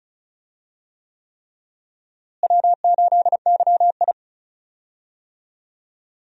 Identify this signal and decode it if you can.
Morse code, heard as a keyed tone, about 36 wpm, 700 Hz. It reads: W8YI
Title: USA Call Signs - 4 Characters - Repeated 3x - 35wpm